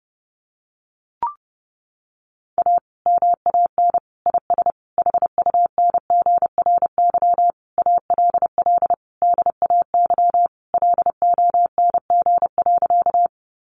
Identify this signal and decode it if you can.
Morse code, heard as a keyed tone, about 30 wpm, 700 Hz. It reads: A MAN IS HUNGRY ALL DAY LONG.
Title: Sentences from Top 1400 Words 30wpm